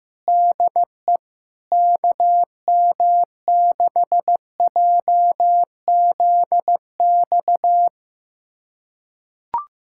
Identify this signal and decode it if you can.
Morse code, heard as a keyed tone, about 16 wpm, 700 Hz. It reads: DE KM6JZX
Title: USA Callsign DE Callsign - 6 Characters - 15wpm